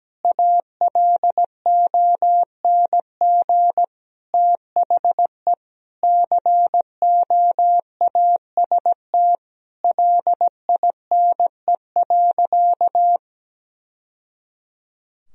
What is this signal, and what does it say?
Morse code, heard as a keyed tone, about 17 wpm, 700 Hz. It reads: ALONG THE COAST LINE.
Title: Sentences from Top 1000 Words - Review Part 4 - 17wpm